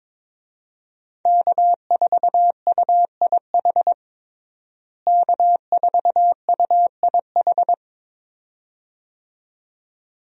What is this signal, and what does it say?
Morse code, heard as a keyed tone, about 22 wpm, 700 Hz. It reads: K4UIH K4UIH
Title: USA Call Signs - 5 Characters - Repeated 2x - 22wpm